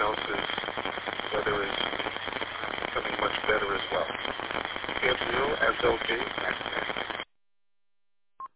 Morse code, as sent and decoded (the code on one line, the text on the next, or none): none